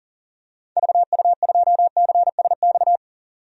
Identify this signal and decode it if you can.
Morse code, heard as a keyed tone, about 40 wpm, 700 Hz. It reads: VU2CSX
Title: DXCC-100 Call Signs - 6 Characters - 40wpm